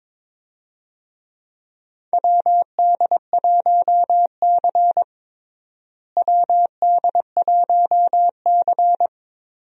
Morse code, heard as a keyed tone, about 21 wpm, 700 Hz. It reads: WD1C WD1C
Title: USA Call Signs - 4 Characters - Repeated 2x - 22wpm